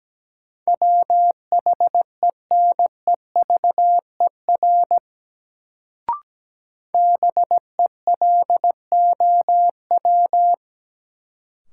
Morse code, heard as a keyed tone, about 17 wpm, 700 Hz. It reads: WHENEVER BELOW
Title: Sentences from Top 1200 Words - New Word Drill Alt - 17wpm